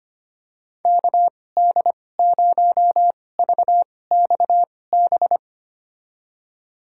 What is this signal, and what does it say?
Morse code, heard as a keyed tone, about 25 wpm, spaced as for 18 wpm, 700 Hz. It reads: KD0VXB
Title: USA Call Signs - 6 Characters - 2x Character Spacing - 25wpm